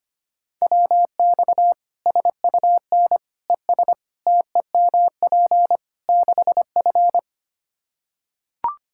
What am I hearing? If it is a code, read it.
Morse code, heard as a keyed tone, about 25 wpm, 700 Hz. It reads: WX SUN ES TEMP 6F
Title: Practice QSO - Exchange 2 - Info - 25wpm